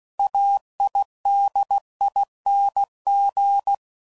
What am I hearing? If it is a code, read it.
Morse code, heard as a keyed tone, about 16 wpm, 800 Hz. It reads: AIDING
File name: Suffixes / Special INGs / Sngl Ings